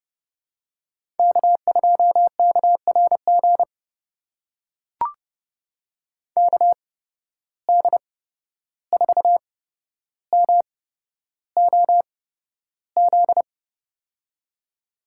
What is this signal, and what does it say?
Morse code, heard as a keyed tone, about 31 wpm, spaced as for 12 wpm, 700 Hz. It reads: K2KRG KD4MOZ
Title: USA Call Signs - 8x Character Spacing - 30wpm